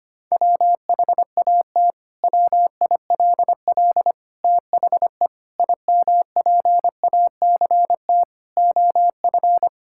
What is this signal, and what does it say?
Morse code, heard as a keyed tone, about 25 wpm, 700 Hz. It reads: WHAT WILL THE IMPACT OF
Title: Sentences from Top 1000 Words - Review Part 6 - 25wpm